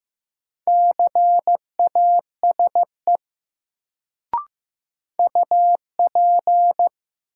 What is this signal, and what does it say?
Morse code, heard as a keyed tone, about 15 wpm, 700 Hz. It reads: CASE UP
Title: Sets of 2 Words - New Word Drill - 15wpm